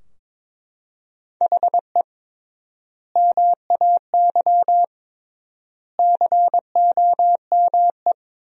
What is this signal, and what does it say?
Morse code, heard as a keyed tone, about 22 wpm, 700 Hz. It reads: HE MAY COME
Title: 3x Word-Spacing - Sentences from Top 300 Words - 22wpm